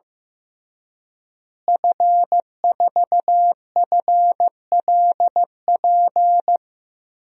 Morse code, heard as a keyed tone, about 16 wpm, 700 Hz. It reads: F4FLP
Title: DXCC-100 Call Signs - 5 Characters - Repeated 3x - 15wpm